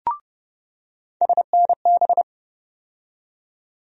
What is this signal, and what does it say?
Morse code, heard as a keyed tone, about 31 wpm, 700 Hz. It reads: SNB
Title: USA Call Sign Postfixes - 3 Letters 30wpm